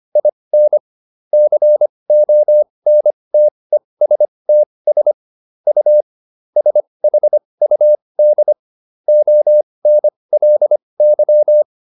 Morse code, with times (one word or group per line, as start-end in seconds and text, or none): 0.2-0.8 IN
1.3-5.1 CONTESTS
5.7-6.0 U
6.6-8.5 SHUD
9.1-11.6 ONLY